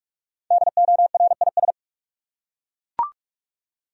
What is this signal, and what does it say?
Morse code, heard as a keyed tone, about 42 wpm, 700 Hz. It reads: DORIS
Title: USA Common Names - Max 5 Letters - 45wpm